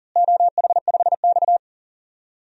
Morse code, heard as a keyed tone, about 39 wpm, 700 Hz. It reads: OH5X
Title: DXCC-100 Call Signs - 4 Characters - Repeated 2x - 40wpm